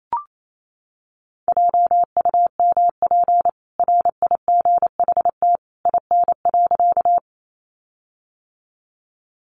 Morse code, heard as a keyed tone, about 28 wpm, 700 Hz. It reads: JUMP RIGHT IN.
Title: Sentences from Top 1300 Words 28wpm